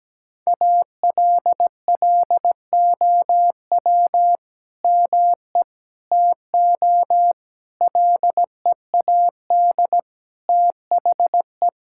Morse code, heard as a keyed tone, about 17 wpm, 700 Hz. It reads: ALLOW ME TO LEAD THE
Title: Sentences from Top 1000 Words - Review Part 5 - 17wpm